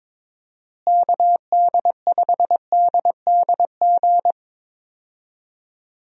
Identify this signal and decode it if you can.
Morse code, heard as a keyed tone, about 22 wpm, 700 Hz. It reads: KD5DDG
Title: USA Call Signs - 6 Characters - Repeated 2x - 22wpm